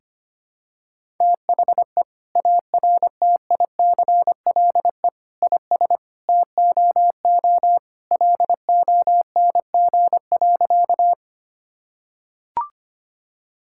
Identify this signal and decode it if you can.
Morse code, heard as a keyed tone, about 25 wpm, 700 Hz. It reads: THE ARTICLE IS TOO LONG.
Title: Sentences from Top 1000 Words - Review Part 5 - 25wpm